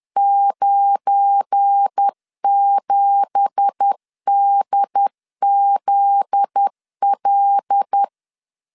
Morse code, heard as a keyed tone, about 10 wpm, 800 Hz. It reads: 97DZL
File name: LEZIONI / 10 SIGNS / 10A SLASH 10wpm 2808